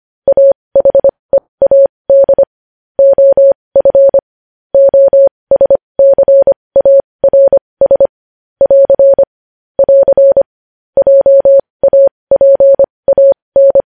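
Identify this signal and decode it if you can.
Morse code, heard as a keyed tone, about 25 wpm, spaced as for 20 wpm, 550 Hz. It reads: AHEAD OF OSCARS <AR> <AR> JAPAN
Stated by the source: Morse code news headlines practice at 20.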